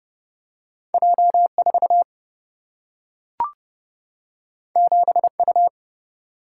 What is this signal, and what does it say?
Morse code, heard as a keyed tone, about 30 wpm, 700 Hz. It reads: J 4 7 U
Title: Single-Letter Single-Number 30wpm